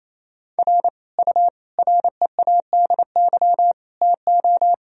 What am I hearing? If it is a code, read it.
Morse code, heard as a keyed tone, about 28 wpm, 700 Hz. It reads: R U READY TO
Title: Sentences from Top 400 Words and Abbreviations 28wpm